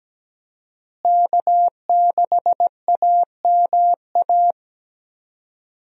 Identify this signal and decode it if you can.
Morse code, heard as a keyed tone, about 17 wpm, 700 Hz. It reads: K6AMA
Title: USA Call Signs - 5 Characters - Repeated 2x - 17wpm